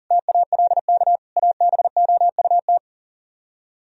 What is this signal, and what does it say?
Morse code, heard as a keyed tone, about 55 wpm, spaced as for 48 wpm, 700 Hz. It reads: TALK ABOUT
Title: Speed-Racing - Sets of 5 Words 40wpm